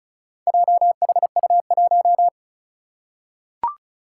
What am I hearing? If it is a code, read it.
Morse code, heard as a keyed tone, about 36 wpm, 700 Hz. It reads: JHU1
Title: Four Challenging Letter-Number Combinations 35wpm